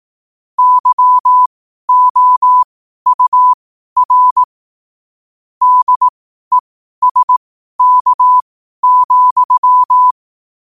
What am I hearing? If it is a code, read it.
Morse code, heard as a keyed tone, about 18 wpm, spaced as for 12 wpm, 1000 Hz. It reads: YOUR DESK,
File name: qotdfiles / 12wpm